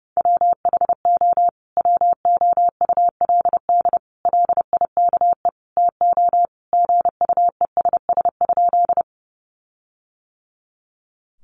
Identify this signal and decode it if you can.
Morse code, heard as a keyed tone, about 30 wpm, 700 Hz. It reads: WHO WOULD LIKE TO GUESS?
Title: Sentences from Top 1500 Words 30wpm